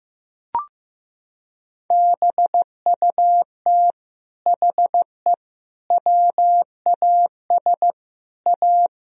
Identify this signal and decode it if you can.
Morse code, heard as a keyed tone, about 15 wpm, 700 Hz. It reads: BUT HE WAS A
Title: Sentences from Top 1000 Words - Review Part 3 - 15wpm